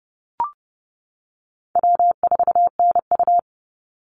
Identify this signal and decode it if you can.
Morse code, heard as a keyed tone, about 30 wpm, 700 Hz. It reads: W4NU
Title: Call Signs - 30wpm